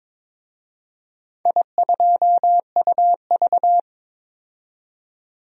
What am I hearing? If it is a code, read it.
Morse code, heard as a keyed tone, about 23 wpm, 700 Hz. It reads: I2UV
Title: DXCC-100 Call Signs - 4 Characters - 22wpm